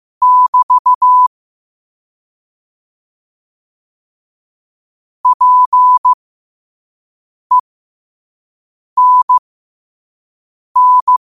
Morse code, heard as a keyed tone, about 15 wpm, spaced as for 5 wpm, 1000 Hz. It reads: <BT> PENN
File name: qotdfiles / 5wpm